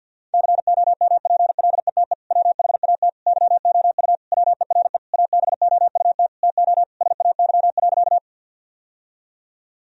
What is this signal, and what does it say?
Morse code, heard as a keyed tone, about 51 wpm, 700 Hz. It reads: COMPLETE WHAT YOU WERE ABOUT TO SAY.
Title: Sentences from Top 1000 Words - Review Part 2 - 50wpm